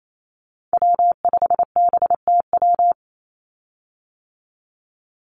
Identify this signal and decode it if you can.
Morse code, heard as a keyed tone, about 28 wpm, 700 Hz. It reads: W5BTW
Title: USA Call Signs 28wpm